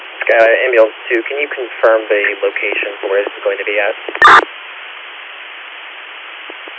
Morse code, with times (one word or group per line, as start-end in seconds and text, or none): none